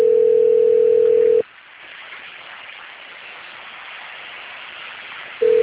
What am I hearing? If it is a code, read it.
Ringback tone (North America), 440 + 480 Hz.